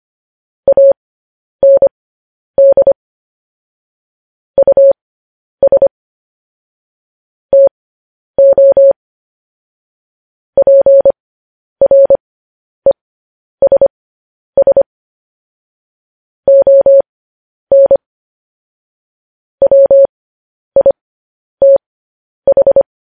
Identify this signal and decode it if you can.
Morse code, heard as a keyed tone, about 25 wpm, spaced as for 10 wpm, 550 Hz. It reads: AND US TO PRESS ON WITH
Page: news headlines in morse code at 10 wpm